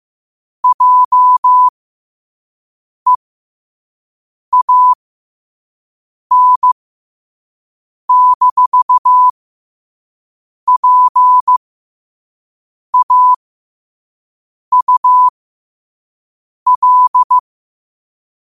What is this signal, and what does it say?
Morse code, heard as a keyed tone, about 15 wpm, spaced as for 5 wpm, 1000 Hz. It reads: JEAN-PAUL
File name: qotdfiles / 5wpm